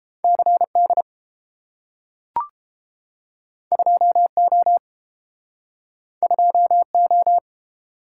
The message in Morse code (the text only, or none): CD 2O 2O